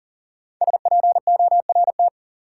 Morse code, heard as a keyed tone, about 38 wpm, 700 Hz. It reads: SPORT
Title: Sentences from Top 1500 Words - New Word Drill Alt - 40wpm